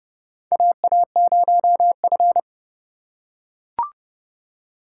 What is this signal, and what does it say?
Morse code, heard as a keyed tone, about 30 wpm, 700 Hz. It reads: AA0F
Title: USA Call Signs - 4 Characters - 2x Character Spacing - 30wpm